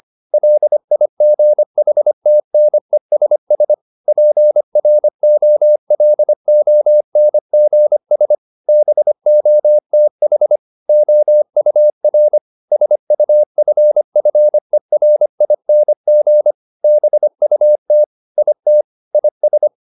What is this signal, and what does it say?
Morse code, heard as a keyed tone, about 25 wpm, 600 Hz. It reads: LIGHTNESS PROLONGS BOTH OUR SUFFERING BUT IT IS